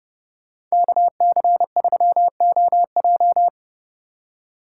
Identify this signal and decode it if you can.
Morse code, heard as a keyed tone, about 38 wpm, 700 Hz. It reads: KC3OJ